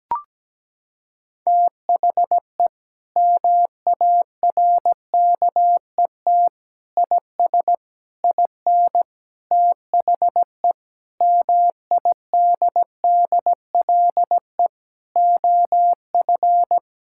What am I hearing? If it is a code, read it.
Morse code, heard as a keyed tone, about 17 wpm, 700 Hz. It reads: THE MARKET IS IN THE MIDDLE OF
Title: Sentences from Top 1000 Words - Review Part 6 - 17wpm